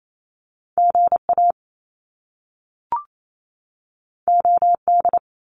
Morse code, heard as a keyed tone, about 28 wpm, 700 Hz. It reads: GA OD
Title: Top Two Letter Combinations in Words 28wpm